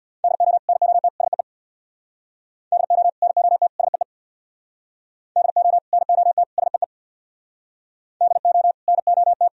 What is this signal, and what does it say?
Morse code, heard as a keyed tone, about 63 wpm, 700 Hz. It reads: DO NOT SEE DO NOT SEE DO NOT SEE DO NOT